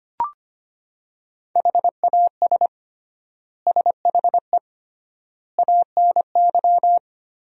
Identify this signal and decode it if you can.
Morse code, heard as a keyed tone, about 25 wpm, 700 Hz. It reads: HAS SHE ANY
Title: 3x Word-Spacing - Sentences from Top 400 Words - 25wpm